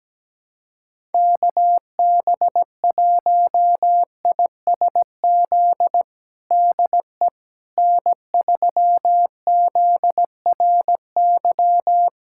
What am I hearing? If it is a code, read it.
Morse code, heard as a keyed tone, about 17 wpm, 700 Hz. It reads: KB1ISZ DE N3ZRY
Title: USA Callsign DE Callsign 17wpm